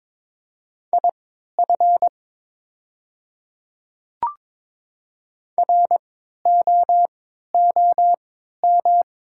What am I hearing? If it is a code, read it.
Morse code, heard as a keyed tone, about 22 wpm, spaced as for 13 wpm, 700 Hz. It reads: IF ROOM
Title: 3x Character Spacing - Top 300 Words - 22wpm